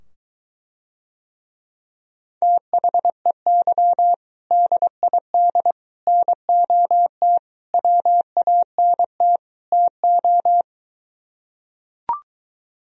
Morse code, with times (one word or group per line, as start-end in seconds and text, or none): 2.4-4.1 THEY
4.5-5.7 DID
6.1-7.4 NOT
7.7-9.4 WANT
9.7-10.6 TO